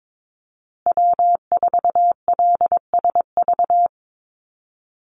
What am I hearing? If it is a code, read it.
Morse code, heard as a keyed tone, about 22 wpm, 700 Hz. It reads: W4LSV
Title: Call Signs Repeated 2x - 22wpm